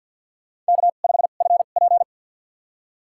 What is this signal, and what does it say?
Morse code, heard as a keyed tone, about 53 wpm, spaced as for 37 wpm, 700 Hz. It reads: K5FP
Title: USA Call Signs - 4 Characters - 2x Character Spacing - 50wpm